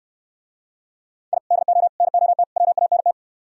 Morse code, heard as a keyed tone, about 64 wpm, 700 Hz. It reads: I DO NOT WANT